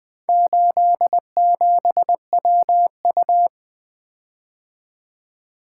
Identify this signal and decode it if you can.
Morse code, heard as a keyed tone, about 23 wpm, 700 Hz. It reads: 87WU
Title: Speed-Racing - Four Letter-Number Combinations - 20wpm